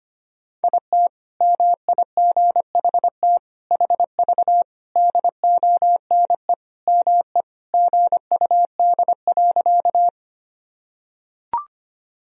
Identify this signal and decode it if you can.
Morse code, heard as a keyed tone, about 25 wpm, 700 Hz. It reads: IT MIGHT HV DONE ME GUD.
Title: Sentences from Top 500 Words and Abbreviations 25wpm